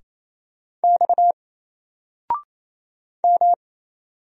Morse code, heard as a keyed tone, about 27 wpm, 700 Hz. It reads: X M
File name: Single-Letter-Number-28wpm